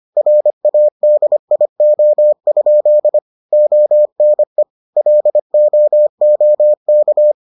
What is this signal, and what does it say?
Morse code, heard as a keyed tone, about 25 wpm, 600 Hz. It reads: RADIO? ONE LOOK